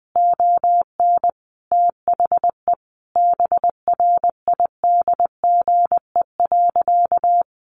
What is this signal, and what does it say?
Morse code, heard as a keyed tone, about 20 wpm, 700 Hz. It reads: ON THE BRIDGE.
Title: Sentences from Top 1200 Words 20wpm